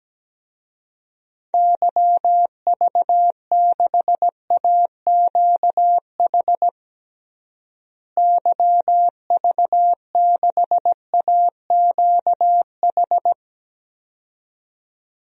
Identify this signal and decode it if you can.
Morse code, heard as a keyed tone, about 17 wpm, 700 Hz. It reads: YV6AQH YV6AQH
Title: DXCC-100 Call Signs - 6 Characters - Repeated 2x - 17wpm